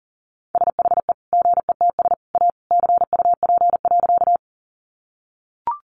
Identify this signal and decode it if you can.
Morse code, heard as a keyed tone, about 40 wpm, 700 Hz. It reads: SHE GETS A CUP.
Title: Additional Easy Phrases 40wpm